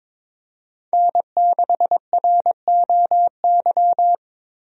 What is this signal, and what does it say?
Morse code, heard as a keyed tone, about 22 wpm, 700 Hz. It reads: N6ROY
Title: USA Call Signs - 5 Characters - Repeated 3x - 22wpm